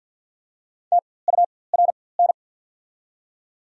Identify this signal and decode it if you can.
Morse code, heard as a keyed tone, about 52 wpm, spaced as for 23 wpm, 700 Hz. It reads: TURN
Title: 4x Character Spacing - Top 500 Words - 50wpm